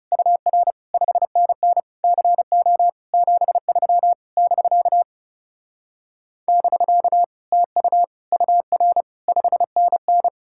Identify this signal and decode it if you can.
Morse code, heard as a keyed tone, about 39 wpm, 700 Hz. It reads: UR 5NN CO 73 <BK> <BK> TU UR 5NN